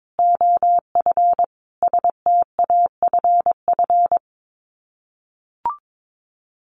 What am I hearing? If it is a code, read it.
Morse code, heard as a keyed tone, about 22 wpm, 700 Hz. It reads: OF STAFF
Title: Sets of 5 Words Encore2 22wpm